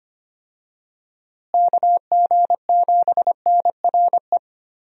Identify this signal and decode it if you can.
Morse code, heard as a keyed tone, about 25 wpm, 700 Hz. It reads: KG7NRE
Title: USA Call Signs - 6 Characters - Repeated 3x - 25wpm